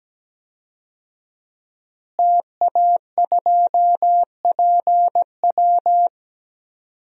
Morse code, heard as a keyed tone, about 17 wpm, 700 Hz. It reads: TA2PW
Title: DXCC-100 Call Signs - 5 Characters - Repeated 2x - 17wpm